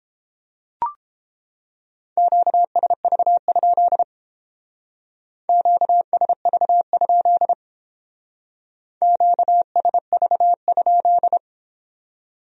Morse code, heard as a keyed tone, about 31 wpm, 700 Hz. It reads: QSV? QSV? QSV?